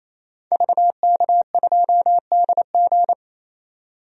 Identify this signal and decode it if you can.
Morse code, heard as a keyed tone, about 28 wpm, 700 Hz. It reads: VK2DG
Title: DXCC-100 Call Signs - 5 Characters - 28wpm